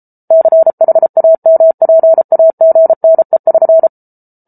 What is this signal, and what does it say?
Morse code, heard as a keyed tone, about 34 wpm, 650 Hz. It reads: CHAMPAGNE<SN>